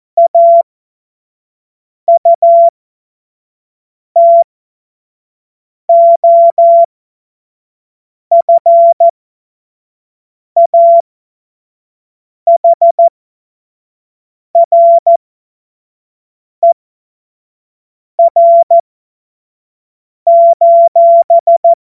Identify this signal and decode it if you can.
Morse code, heard as a keyed tone, about 14 wpm, spaced as for 5 wpm, 680 Hz. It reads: AUTOFAHRER:
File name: Morsekurs / Witze / Witz 07. März 2023 25 F70